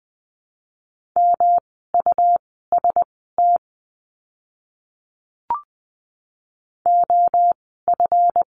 Morse code, heard as a keyed tone, about 20 wpm, spaced as for 14 wpm, 700 Hz. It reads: MUST OF
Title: 237 Lesson 38f - Letter q - Words 20wpm